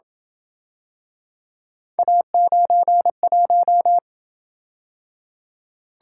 Morse code, heard as a keyed tone, about 27 wpm, 700 Hz. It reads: A91